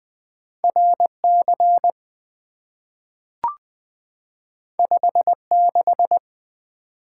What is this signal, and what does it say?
Morse code, heard as a keyed tone, about 20 wpm, 700 Hz. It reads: RC 56
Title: Two Letter-Number Combinations - 20wpm